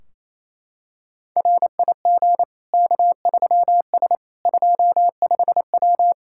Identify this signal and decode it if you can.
Morse code, heard as a keyed tone, about 28 wpm, 700 Hz. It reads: RIG K3S 25W
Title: K1USN SST QSO Practice 28wpm